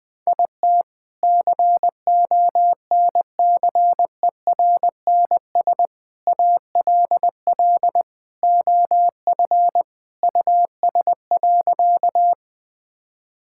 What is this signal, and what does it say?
Morse code, heard as a keyed tone, about 20 wpm, 700 Hz. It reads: IT CONCERNS ALL OF US.
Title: Sentences with 3-Word Phrases - 20wpm